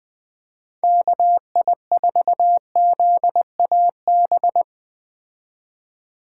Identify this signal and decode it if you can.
Morse code, heard as a keyed tone, about 21 wpm, 700 Hz. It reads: KI4ZAB